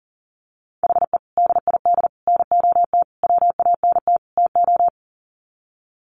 Morse code, heard as a keyed tone, about 40 wpm, 700 Hz. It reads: HE DID NOT WANT TO
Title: Sets of 5 Words 40wpm